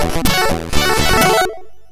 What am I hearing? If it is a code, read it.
There is no code.